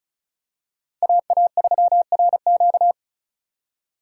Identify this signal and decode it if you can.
Morse code, heard as a keyed tone, about 36 wpm, 700 Hz. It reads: AA3RQ
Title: USA Call Signs - 5 Characters - 2x Character Spacing - 35wpm